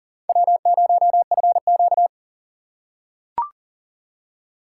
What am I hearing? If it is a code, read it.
Morse code, heard as a keyed tone, about 38 wpm, 700 Hz. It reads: W0FQ
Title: USA Call Signs - 4 Characters - 2x Character Spacing - 40wpm